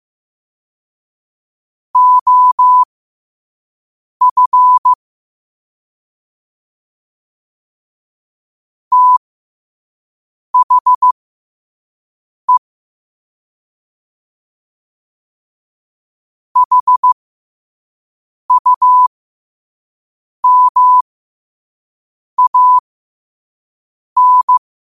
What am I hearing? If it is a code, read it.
Morse code, heard as a keyed tone, about 15 wpm, spaced as for 5 wpm, 1000 Hz. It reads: OF THE HUMAN
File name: qotdfiles / 5wpm